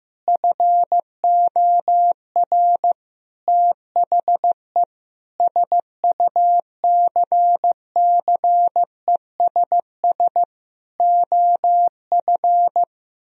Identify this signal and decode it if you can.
Morse code, heard as a keyed tone, about 15 wpm, 700 Hz. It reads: FOR THE SUCCESS OF
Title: Sentences from Top 1000 Words - Review Part 4 - 15wpm